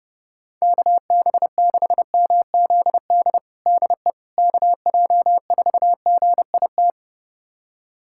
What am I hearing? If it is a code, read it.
Morse code, heard as a keyed tone, about 30 wpm, 700 Hz. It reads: KB6MZD DE KJ4GIT
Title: USA Callsign DE Callsign - 6 Characters - 30wpm